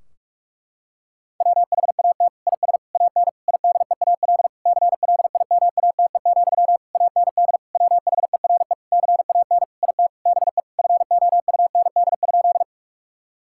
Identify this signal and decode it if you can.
Morse code, heard as a keyed tone, about 46 wpm, 700 Hz. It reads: WHAT IS AN IDEAL CLIMATE, AND WHERE CAN IT BE FOUND?
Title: Sentences from Top 1600 Words 45wpm